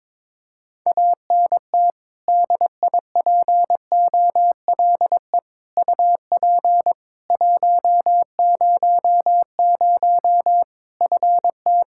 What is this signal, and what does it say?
Morse code, heard as a keyed tone, about 21 wpm, 700 Hz. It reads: ANT DIPOLE UP 100 FT
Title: Practice QSO - Exchange 2 - Info - 22wpm